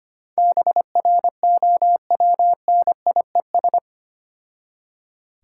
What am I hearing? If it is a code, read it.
Morse code, heard as a keyed tone, about 25 wpm, 700 Hz. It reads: BROWNIES
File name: practice / 25wpm / Difficult Words with Hints - 25wpm